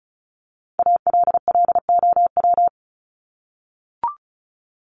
Morse code, heard as a keyed tone, about 34 wpm, 700 Hz. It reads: ALLOW
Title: Top 501-750 Words 35wpm